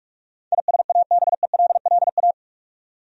Morse code, heard as a keyed tone, about 42 wpm, 700 Hz. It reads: ISABELLA
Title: USA Common Names - 45wpm